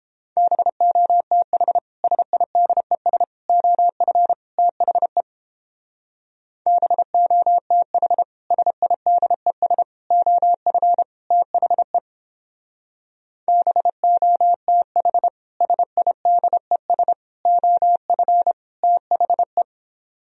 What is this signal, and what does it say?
Morse code, heard as a keyed tone, about 31 wpm, 700 Hz. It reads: BOTH SIDES OF THE BOTH SIDES OF THE BOTH SIDES OF THE